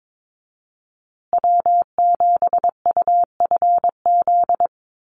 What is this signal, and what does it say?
Morse code, heard as a keyed tone, about 22 wpm, 700 Hz. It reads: W7UFZ
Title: USA Call Signs Repeated 3x 22wpm